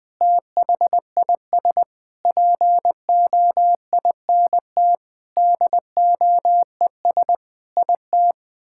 Morse code, heard as a keyed tone, about 21 wpm, 700 Hz. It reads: THIS POINT DOES IT
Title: Sentences with 8-Word Phrases - 20wpm